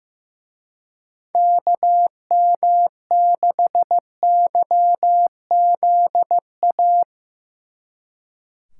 Morse code, heard as a keyed tone, about 17 wpm, 700 Hz. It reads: KM6YZA
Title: Speed Racing - USA Call Signs 15wpm